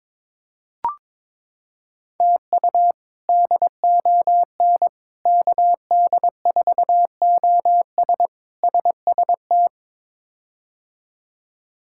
Morse code, heard as a keyed tone, about 22 wpm, 700 Hz. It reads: TU DON KD4OS SST
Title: K1USN SST QSO Practice - 22wpm